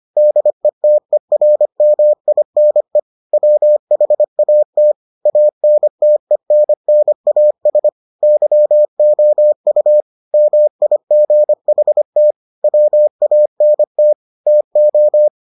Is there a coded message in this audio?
Yes: Morse code from 0.2 to 15.3 s, heard as a keyed tone, about 25 wpm, 600 Hz. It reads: DETERMINE WHAT ANTENNAS YOU MIGHT WANT TO